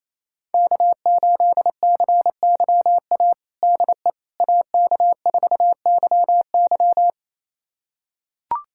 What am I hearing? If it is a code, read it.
Morse code, heard as a keyed tone, about 28 wpm, 700 Hz. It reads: K8CYA DE AK4YY
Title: USA Callsign DE Callsign - 5 Characters - 28wpm